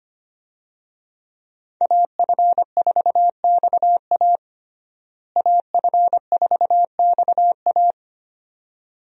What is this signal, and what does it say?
Morse code, heard as a keyed tone, about 25 wpm, 700 Hz. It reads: AF4XA AF4XA
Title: USA Call Signs - 5 Characters - Repeated 2x - 25wpm